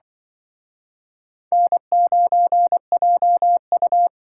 Morse code, heard as a keyed tone, about 24 wpm, 700 Hz. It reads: N9JU
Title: Speed Racing - USA 1x2 Call Signs 20wpm